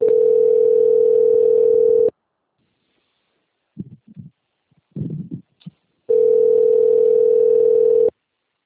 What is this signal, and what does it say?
Ringback tone (North America), 440 + 480 Hz.
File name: out- - - - - .17652